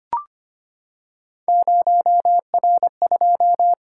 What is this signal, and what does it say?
Morse code, heard as a keyed tone, about 25 wpm, 700 Hz. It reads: 0R2
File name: Three Letter-Number Combinations - 25wpm